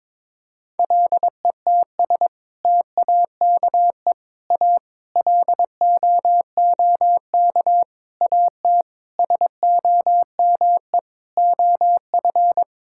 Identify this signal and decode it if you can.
Morse code, heard as a keyed tone, about 22 wpm, 700 Hz. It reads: LETS TAKE A LOOK AT SOME OF
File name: practice / 22wpm / Sets of 8 Words - 22wpm